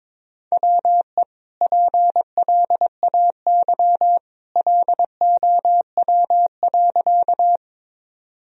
Morse code, heard as a keyed tone, about 22 wpm, 700 Hz. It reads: WE PLAY LOW.
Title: Sentences from Top 1000 Words - Review Part 1 - 22wpm